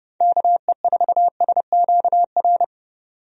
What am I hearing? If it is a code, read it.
Morse code, heard as a keyed tone, about 32 wpm, 700 Hz. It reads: KE4SQR
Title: USA Call Signs - 6 Characters - 30wpm